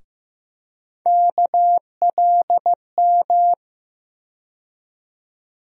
Morse code, heard as a keyed tone, about 15 wpm, 700 Hz. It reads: KLM
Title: USA Call Sign Postfixes - 3 Letters 15wpm